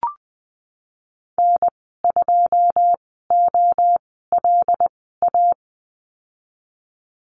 Morse code, heard as a keyed tone, about 20 wpm, spaced as for 15 wpm, 700 Hz. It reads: N2OLA
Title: 088 Lesson 15g - Number 5 - Call Signs 20wpm